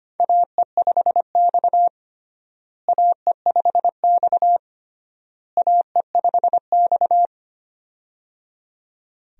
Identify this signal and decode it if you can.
Morse code, heard as a keyed tone, about 25 wpm, 700 Hz. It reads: AE5X AE5X AE5X
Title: USA Call Signs - 4 Characters - Repeated 3x - 25wpm